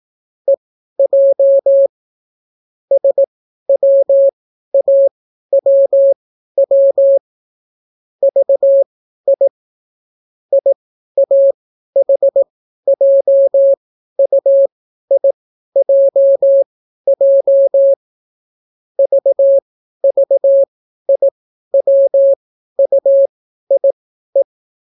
Morse code, heard as a keyed tone, about 18 wpm, spaced as for 12 wpm, 550 Hz. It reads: EJ SWAWW VI IAHJUIJJ VVIWUIE